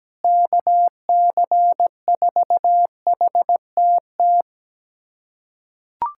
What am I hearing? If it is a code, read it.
Morse code, heard as a keyed tone, about 17 wpm, 700 Hz. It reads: KC4HTT
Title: USA Callsign DE Callsign 17wpm